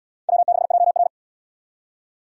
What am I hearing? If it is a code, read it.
Morse code, heard as a keyed tone, about 73 wpm, 700 Hz. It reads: W6JR